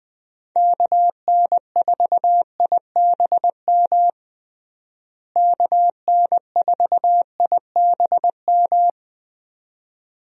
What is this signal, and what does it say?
Morse code, heard as a keyed tone, about 20 wpm, 700 Hz. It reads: KN4IBM KN4IBM
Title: USA Call Signs - 6 Characters - Repeated 2x - 20wpm